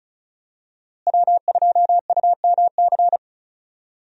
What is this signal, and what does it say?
Morse code, heard as a keyed tone, about 36 wpm, 700 Hz. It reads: W2UMC
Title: USA Call Signs - 5 Characters - Repeated 3x - 35wpm